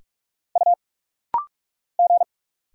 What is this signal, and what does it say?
Morse code, heard as a keyed tone, about 45 wpm, 690 Hz. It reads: U G
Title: Single Letters - Rapid-Fire 45wpm